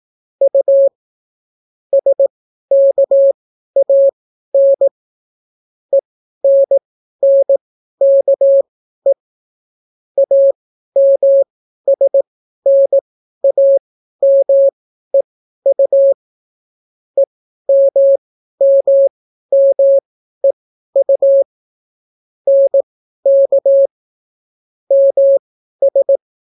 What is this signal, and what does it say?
Morse code, heard as a keyed tone, about 18 wpm, spaced as for 12 wpm, 550 Hz. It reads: U SKAN ENNKE AMSNAMEU EMMMEU NK MS